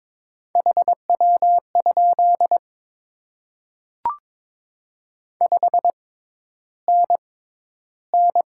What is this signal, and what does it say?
Morse code, heard as a keyed tone, about 22 wpm, spaced as for 11 wpm, 700 Hz. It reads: HW? 5NN
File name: Top 20 QSO Elements - 6x Character Spacing - 22wpm